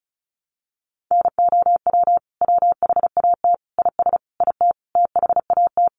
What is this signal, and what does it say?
Morse code, heard as a keyed tone, about 34 wpm, 700 Hz. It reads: NOW WHAT IS IT THAT
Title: Sentences from Top 200 Words 35wpm